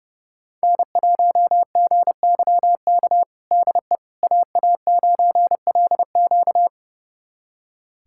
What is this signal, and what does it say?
Morse code, heard as a keyed tone, about 32 wpm, 700 Hz. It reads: N1GYK DE AA9LQ
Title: USA Callsign DE Callsign - 5 Characters - 30wpm